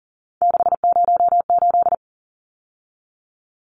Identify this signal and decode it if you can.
Morse code, heard as a keyed tone, about 40 wpm, 700 Hz. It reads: 608
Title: Numbers - 3-Digit 40wpm